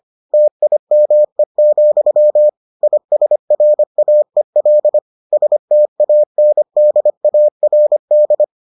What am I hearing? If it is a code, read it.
Morse code, heard as a keyed tone, about 25 wpm, 600 Hz. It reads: TIME, ISRAEL STANDARD